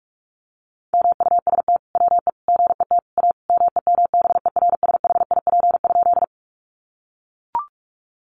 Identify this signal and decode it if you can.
Morse code, heard as a keyed tone, about 44 wpm, 700 Hz. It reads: MUST WE GET A MEMBERSHIP?
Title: Sentences from Top 1400 Words 45wpm